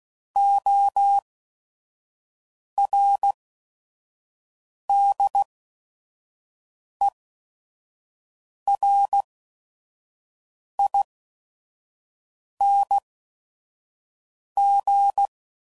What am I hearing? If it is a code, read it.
Morse code, heard as a keyed tone, about 16 wpm, spaced as for 5 wpm, 800 Hz. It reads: ORDERING
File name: Prefixes / By First Letter / Os